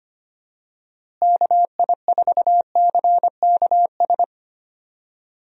Morse code, heard as a keyed tone, about 25 wpm, 700 Hz. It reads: KI4CKS
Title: USA Call Signs - 6 Characters - Repeated 3x - 25wpm